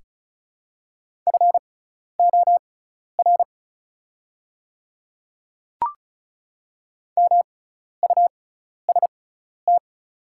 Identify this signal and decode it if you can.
Morse code, heard as a keyed tone, about 36 wpm, spaced as for 12 wpm, 690 Hz. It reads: FOR MUST
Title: 6x Character Spacing - Top 300 Words - 35wpm